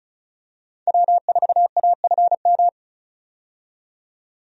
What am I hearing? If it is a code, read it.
Morse code, heard as a keyed tone, about 36 wpm, 700 Hz. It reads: W4AFM
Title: USA Call Signs - 5 Characters - 35wpm